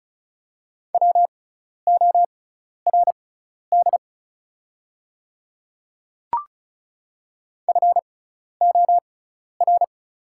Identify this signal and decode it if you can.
Morse code, heard as a keyed tone, about 36 wpm, spaced as for 12 wpm, 700 Hz. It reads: WORD FOR
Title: 6x Character Spacing - Top 200 Words - 35wpm